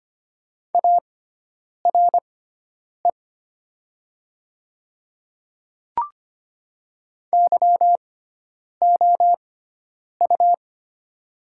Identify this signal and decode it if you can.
Morse code, heard as a keyed tone, about 25 wpm, spaced as for 9 wpm, 700 Hz. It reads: ARE YOU
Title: 6x Character Spacing - Top 100 Words - 25wpm